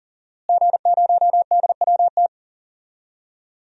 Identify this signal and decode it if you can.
Morse code, heard as a keyed tone, about 37 wpm, 700 Hz. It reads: G0DWT
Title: DXCC-100 Call Signs - 5 Characters - Repeated 2x - 40wpm